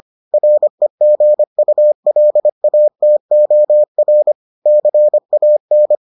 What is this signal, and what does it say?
Morse code, heard as a keyed tone, about 25 wpm, 600 Hz. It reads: REGULATOR CAN